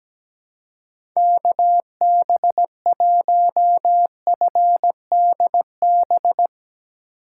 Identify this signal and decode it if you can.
Morse code, heard as a keyed tone, about 17 wpm, 700 Hz. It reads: KB1FDB
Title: USA Call Signs - 2x Character Spacing - 17wpm